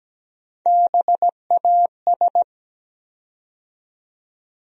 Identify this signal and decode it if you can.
Morse code, heard as a keyed tone, about 17 wpm, 700 Hz. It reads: BAS
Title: USA Call Sign Postfixes - 3 Letters 17wpm